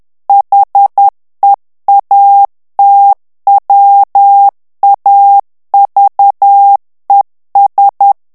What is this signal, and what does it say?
Morse code, heard as a keyed tone, about 10 wpm, 800 Hz. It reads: HEATWAVES
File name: morse mp3 / ANGLAIS texte clair 2 vitesse 10 mots minute